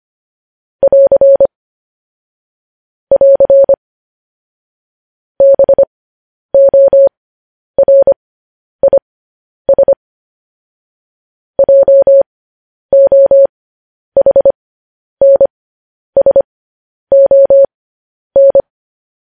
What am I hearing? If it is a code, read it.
Morse code, heard as a keyed tone, about 25 wpm, spaced as for 10 wpm, 550 Hz. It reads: <AR> <AR> BORIS JOHNSON